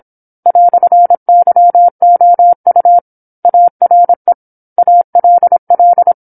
Morse code, heard as a keyed tone, about 27 wpm, 700 Hz. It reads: "YOU ARE ALL
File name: 08-Chapter-08 26 WPM